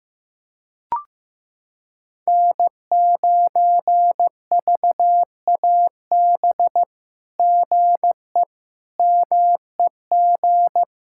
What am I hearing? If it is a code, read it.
Morse code, heard as a keyed tone, about 15 wpm, 700 Hz. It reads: N9VAB GE MEG
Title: SOTA QSO Practice 15wpm